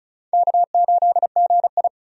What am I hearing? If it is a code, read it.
Morse code, heard as a keyed tone, about 34 wpm, 700 Hz. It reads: K8GI
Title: USA Call Signs - 4 Characters - 35wpm